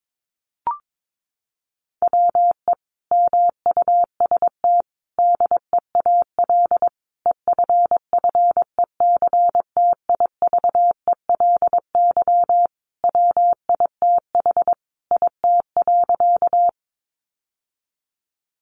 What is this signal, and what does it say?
Morse code, heard as a keyed tone, about 22 wpm, 700 Hz. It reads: WE MUST DEAL EFFECTIVELY WITH IT.
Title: Sentences from Top 1400 Words 22wpm